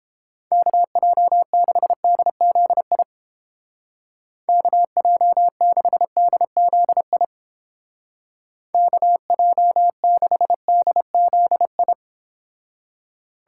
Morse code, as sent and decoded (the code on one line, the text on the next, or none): -.- .--- -.... -.. --.. .. / -.- .--- -.... -.. --.. .. / -.- .--- -.... -.. --.. ..
KJ6DZI KJ6DZI KJ6DZI